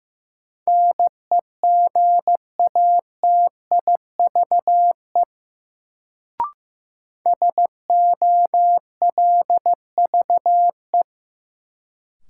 Morse code, heard as a keyed tone, about 15 wpm, 700 Hz. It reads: NEGATIVE SOLVE
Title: Sentences from Top 1000 Words - New Word Drill - 15wpm